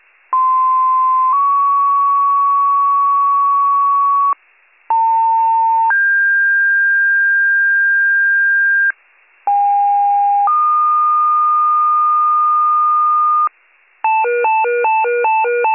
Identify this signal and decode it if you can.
Two-tone page, heard as a one-second tone, then a three-second tone, about 1040 then 1120 Hz, about 900 then 1600 Hz, about 810 then 1220 Hz. It is heard from 0.3 to 13.5 s.